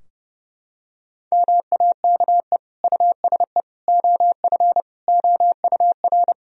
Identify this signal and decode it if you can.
Morse code, heard as a keyed tone, about 30 wpm, 700 Hz. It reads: MAKE USE OF OUR
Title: Sentences from Top 1000 Words - Review Part 4 - 30wpm